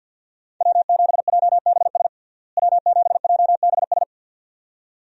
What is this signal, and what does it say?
Morse code, heard as a keyed tone, about 52 wpm, 700 Hz. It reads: W7JBS W7JBS
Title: USA Call Signs - 5 Characters - Repeated 2x - 50wpm